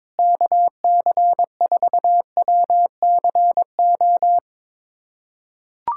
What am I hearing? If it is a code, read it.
Morse code, heard as a keyed tone, about 22 wpm, 700 Hz. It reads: KC4WCO
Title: USA Callsign DE Callsign - 6 Characters - 22wpm